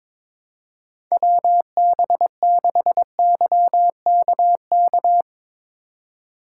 Morse code, heard as a keyed tone, about 22 wpm, 700 Hz. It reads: WB6YKK
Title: USA Call Signs - 6 Characters - 2x Character Spacing - 22wpm